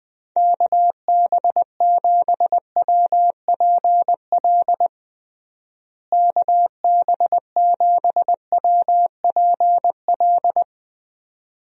Morse code, heard as a keyed tone, about 20 wpm, 700 Hz. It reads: KB7WPL KB7WPL
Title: USA Call Signs - 6 Characters - Repeated 2x - 20wpm